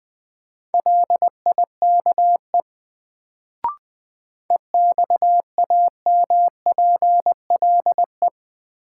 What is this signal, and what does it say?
Morse code, heard as a keyed tone, about 20 wpm, 700 Hz. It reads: LIKE EXAMPLE
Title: Sets of 2 Words - New Word Drill - 20wpm